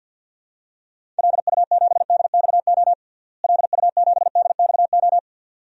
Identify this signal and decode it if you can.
Morse code, heard as a keyed tone, about 50 wpm, 700 Hz. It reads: LU7DXO LU7DXO
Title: DXCC-100 Call Signs - 6 Characters - Repeated 2x - 50wpm